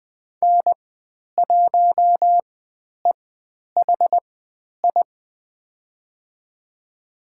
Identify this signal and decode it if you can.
Morse code, heard as a keyed tone, about 20 wpm, spaced as for 10 wpm, 700 Hz. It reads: N1EHI
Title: 102 Lesson 17g - Letter u - Call Signs 20wpm